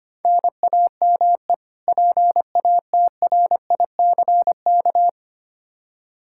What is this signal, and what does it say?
Morse code, heard as a keyed tone, about 25 wpm, 700 Hz. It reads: NAME PATRICK
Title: Practice QSO - Exchange 1 - Beginning - 25wpm